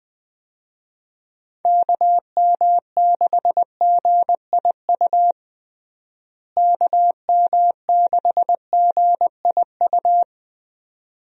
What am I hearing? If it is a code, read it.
Morse code, heard as a keyed tone, about 20 wpm, 700 Hz. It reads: KM6GIU KM6GIU
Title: USA Call Signs - 6 Characters - Repeated 2x - 20wpm